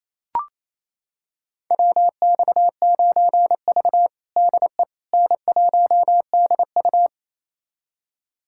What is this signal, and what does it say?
Morse code, heard as a keyed tone, about 28 wpm, 700 Hz. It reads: WX9V DE N1DU
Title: USA Callsign DE Callsign - 4 Characters - 28wpm